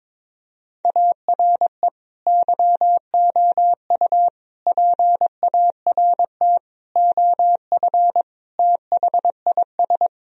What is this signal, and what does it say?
Morse code, heard as a keyed tone, about 21 wpm, 700 Hz. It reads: ARE YOU PART OF THIS
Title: Sentences from Top 1000 Words - Review Part 2 - 22wpm